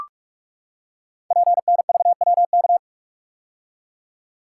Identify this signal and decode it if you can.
Morse code, heard as a keyed tone, about 44 wpm, 690 Hz. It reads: PNVWK
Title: Code Groups - 5 Letters - 45wpm